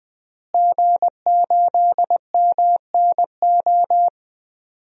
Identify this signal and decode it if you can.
Morse code, heard as a keyed tone, about 21 wpm, 700 Hz. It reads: G8MNO
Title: DXCC-100 Call Signs - 5 Characters - Repeated 2x - 20wpm